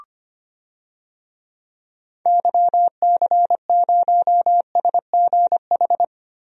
Morse code, heard as a keyed tone, about 25 wpm, 700 Hz. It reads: YC0SGH
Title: DXCC-100 Call Signs - 6 Characters - Repeated 3x - 25wpm